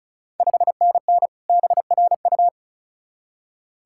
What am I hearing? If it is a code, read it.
Morse code, heard as a keyed tone, about 36 wpm, 690 Hz. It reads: 5NN BRU
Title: Ontario QSO Party Exchange 35wpm